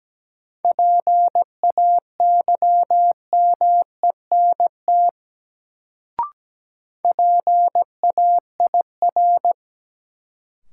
Morse code, heard as a keyed tone, about 17 wpm, 700 Hz. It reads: PAYMENT PAIR
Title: Sentences from Top 1200 Words - New Word Drill - 17wpm